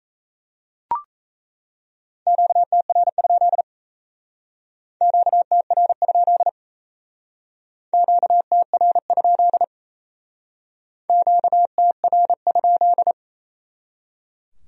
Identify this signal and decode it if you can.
Morse code, heard as a keyed tone, about 38 wpm, 700 Hz. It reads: QTR? QTR? QTR? QTR?